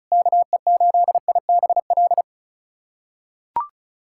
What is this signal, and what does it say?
Morse code, heard as a keyed tone, about 36 wpm, 700 Hz. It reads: KE8IBL
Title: USA Callsign DE Callsign - 6 Characters - 35wpm